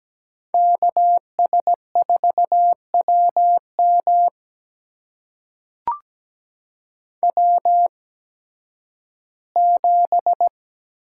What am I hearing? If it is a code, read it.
Morse code, heard as a keyed tone, about 17 wpm, spaced as for 11 wpm, 700 Hz. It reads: KS4WM W7
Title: USA Call Signs - 8x Character Spacing - 17wpm